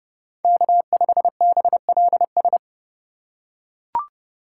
Morse code, heard as a keyed tone, about 30 wpm, 700 Hz. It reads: K5BLS